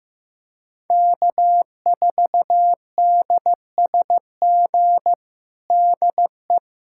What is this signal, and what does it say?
Morse code, heard as a keyed tone, about 16 wpm, 700 Hz. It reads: K4DSG DE
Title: USA Callsign DE Callsign - 5 Characters - 15wpm